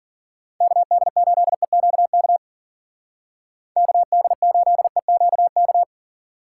Morse code, heard as a keyed tone, about 52 wpm, 700 Hz. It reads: KD8EQK KD8EQK